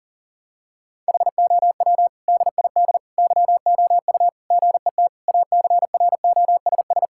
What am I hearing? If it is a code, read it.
Morse code, heard as a keyed tone, about 40 wpm, 700 Hz. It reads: HOW DID YOU GET ACROSS
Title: Sentences from Top 1000 Words - Review Part 5 - 40wpm